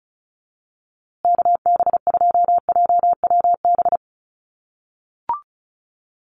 Morse code, heard as a keyed tone, about 34 wpm, 700 Hz. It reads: KB2JWB
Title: USA Call Signs Repeated 3x 35wpm